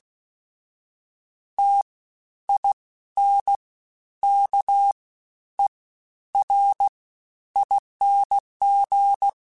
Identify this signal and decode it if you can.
Morse code, heard as a keyed tone, about 16 wpm, spaced as for 10 wpm, 800 Hz. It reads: TINKERING
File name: Special INGs-mp3s / Special INGs-mp3s / Fast ing @15